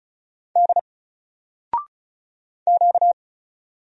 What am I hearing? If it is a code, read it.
Morse code, heard as a keyed tone, about 35 wpm, 700 Hz. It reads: D Q
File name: Single-Letter-Number-35wpm